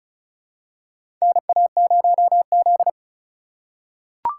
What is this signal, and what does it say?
Morse code, heard as a keyed tone, about 36 wpm, 700 Hz. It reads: NA0Z
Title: USA Call Signs - 4 Characters - 3x Character Spacing - 35wpm